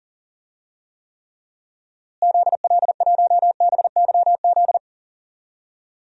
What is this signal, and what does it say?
Morse code, heard as a keyed tone, about 40 wpm, 700 Hz. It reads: ZL1BYZ
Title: DXCC-100 Call Signs - 6 Characters - Repeated 2x - 40wpm